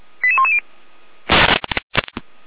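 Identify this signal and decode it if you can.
Five-tone selective call (ZVEI), heard as quick runs of five tones, tones of about 70 ms. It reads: 88110
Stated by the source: ZVEI five-tone selective call received by SDR.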